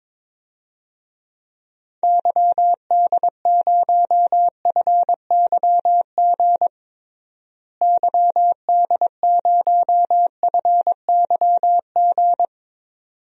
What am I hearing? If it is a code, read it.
Morse code, heard as a keyed tone, about 22 wpm, 700 Hz. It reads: YD0FYG YD0FYG
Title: DXCC-100 Call Signs - 6 Characters - Repeated 2x - 22wpm